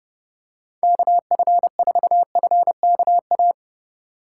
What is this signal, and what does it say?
Morse code, heard as a keyed tone, about 30 wpm, 700 Hz. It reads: KF4FKA
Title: USA Call Signs - 6 Characters - 3x Character Spacing - 30wpm